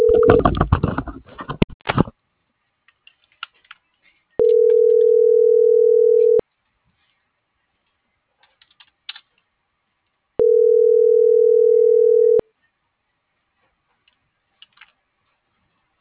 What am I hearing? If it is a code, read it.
Ringback tone (North America), 440 + 480 Hz.